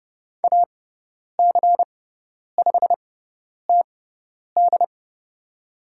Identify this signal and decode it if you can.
Morse code, heard as a keyed tone, about 31 wpm, spaced as for 10 wpm, 700 Hz. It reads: AC5TD
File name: Lesson 16g - Letter c - Call Signs 30wpm